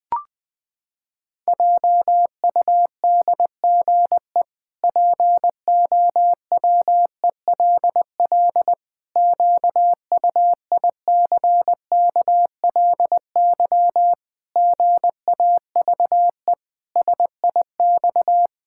Morse code, heard as a keyed tone, about 20 wpm, 700 Hz. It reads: JUDGE POWELL QUICKLY GAVE SIX
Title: Pangram Challenge v2 - 20wpm